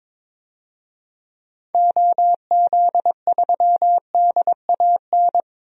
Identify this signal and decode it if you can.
Morse code, heard as a keyed tone, about 22 wpm, 700 Hz. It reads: OZ3DAN
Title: DXCC-100 Call Signs - 6 Characters - 22wpm